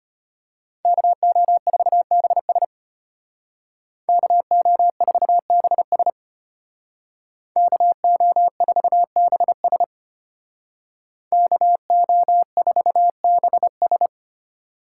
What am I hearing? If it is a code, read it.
Morse code, heard as a keyed tone, about 30 wpm, 700 Hz. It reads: KO4BS KO4BS KO4BS KO4BS
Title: Speed Racing - USA 2x2 Call Signs 25wpm